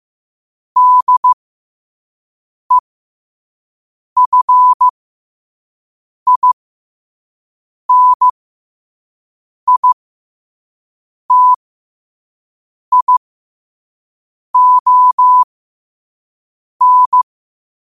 Morse code, heard as a keyed tone, about 15 wpm, spaced as for 5 wpm, 1000 Hz. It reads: DEFINITION
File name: qotdfiles / 5wpm